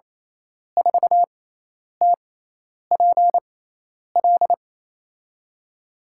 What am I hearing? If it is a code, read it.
Morse code, heard as a keyed tone, about 28 wpm, spaced as for 10 wpm, 700 Hz. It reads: 4TPL
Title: USA Call Signs - 5 Characters - 6x Character Spacing - 28wpm